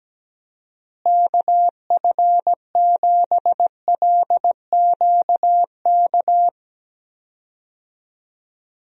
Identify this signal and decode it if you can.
Morse code, heard as a keyed tone, about 17 wpm, 700 Hz. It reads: KF7LQK
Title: USA Call Signs - 6 Characters - Repeated 3x - 17wpm